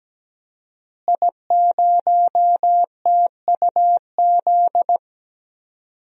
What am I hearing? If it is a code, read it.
Morse code, heard as a keyed tone, about 17 wpm, 700 Hz. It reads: I0TUZ
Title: DXCC-100 Call Signs - 5 Characters - Repeated 3x - 17wpm